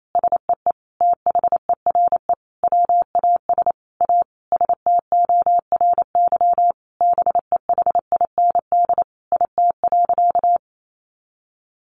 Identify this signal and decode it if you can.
Morse code, heard as a keyed tone, about 28 wpm, 700 Hz. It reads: SEE THERE WAS A STORY BEHIND IT.
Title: Sentences from Top 500 Words 28wpm